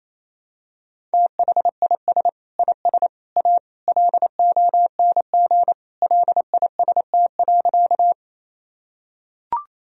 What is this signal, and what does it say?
Morse code, heard as a keyed tone, about 27 wpm, 700 Hz. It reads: THIS IS A LONG LIST.
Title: Sentences from Top 1000 Words - Review Part 2 - 28wpm